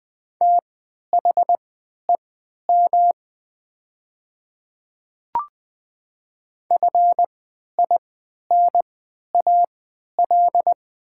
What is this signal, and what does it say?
Morse code, heard as a keyed tone, about 20 wpm, spaced as for 11 wpm, 700 Hz. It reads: THEM FINAL
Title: 3x Character Spacing - Top 500 Words - 20wpm